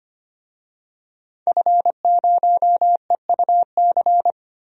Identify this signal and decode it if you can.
Morse code, heard as a keyed tone, about 25 wpm, 700 Hz. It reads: F0EUC
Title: DXCC-100 Call Signs - 5 Characters - 25wpm